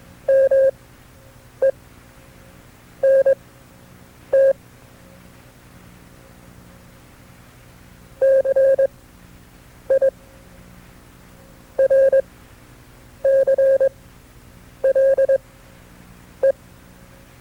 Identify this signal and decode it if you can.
Morse code, heard as a keyed tone, about 22 wpm, spaced as for 7 wpm, 540 Hz. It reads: MENT CIRCLE
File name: FreeFiles / CW-practice / KB6MT-tape2-side2